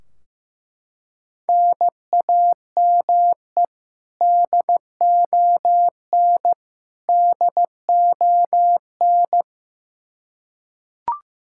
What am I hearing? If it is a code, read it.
Morse code, heard as a keyed tone, about 15 wpm, 700 Hz. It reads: NAME DON DON
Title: Practice QSO - Exchange 1 - Beginning - 15wpm